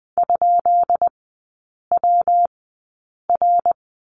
Morse code, heard as a keyed tone, about 20 wpm, spaced as for 8 wpm, 690 Hz. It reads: ?WR
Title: 159 Lesson 26e - Letter g - Three Characters 20wpm